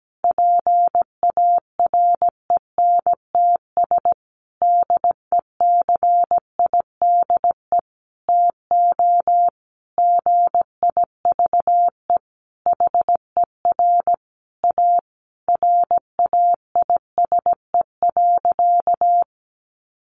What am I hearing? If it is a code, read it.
Morse code, heard as a keyed tone, about 17 wpm, 700 Hz. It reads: PARENTS DECIDE TO GIVE HER A RAISE.
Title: Easy Phrases 17wpm